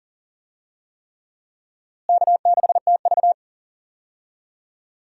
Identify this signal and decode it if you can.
Morse code, heard as a keyed tone, about 40 wpm, 700 Hz. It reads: K6TV
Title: USA Call Signs - 4 Characters - Repeated 3x - 40wpm